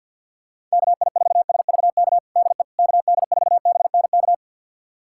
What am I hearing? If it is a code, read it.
Morse code, heard as a keyed tone, about 52 wpm, 700 Hz. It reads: KI4SVK DE KD4BNK
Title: USA Callsign DE Callsign - 6 Characters - 50wpm